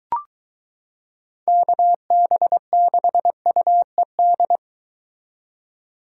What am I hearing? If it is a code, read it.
Morse code, heard as a keyed tone, about 22 wpm, 700 Hz. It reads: KB6UED